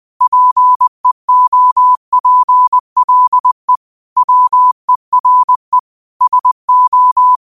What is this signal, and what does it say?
Morse code, heard as a keyed tone, about 20 wpm, 1000 Hz. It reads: PEOPLE WERE SO
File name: qotdfiles / 20wpm